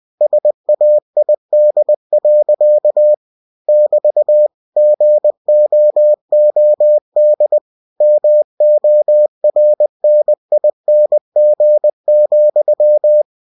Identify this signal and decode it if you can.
Morse code, heard as a keyed tone, about 20 wpm, 600 Hz. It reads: SAID. <BT> GOOD MORNING,
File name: cw-books / winnie-the-pooh / winnie-20wpm-page-0050